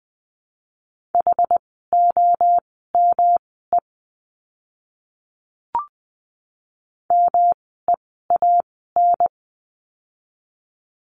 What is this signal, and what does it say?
Morse code, heard as a keyed tone, about 20 wpm, spaced as for 15 wpm, 700 Hz. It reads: HOME MEAN
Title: 108 Lesson 18f - Letter m - Words 20wpm